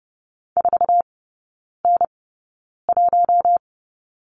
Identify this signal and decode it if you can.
Morse code, heard as a keyed tone, about 30 wpm, spaced as for 9 wpm, 700 Hz. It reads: 4N1
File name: Lesson 17e - Letter u - Three Characters 30wpm